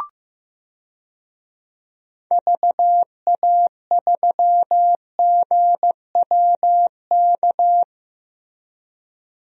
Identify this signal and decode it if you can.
Morse code, heard as a keyed tone, about 15 wpm, 700 Hz. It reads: VA3GWK
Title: DXCC-100 Call Signs - 6 Characters - 15wpm